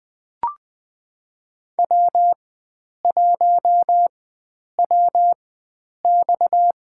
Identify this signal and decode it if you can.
Morse code, heard as a keyed tone, about 20 wpm, spaced as for 9 wpm, 700 Hz. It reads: W1WX
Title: USA Call Signs - 4 Characters - 4x Character Spacing - 20wpm